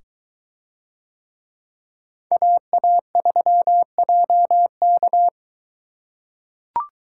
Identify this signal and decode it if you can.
Morse code, heard as a keyed tone, about 22 wpm, 700 Hz. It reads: AA3JK